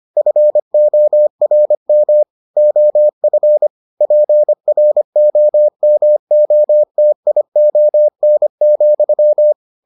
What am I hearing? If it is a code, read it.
Morse code, heard as a keyed tone, about 25 wpm, 600 Hz. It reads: FORM OF PROMOTION,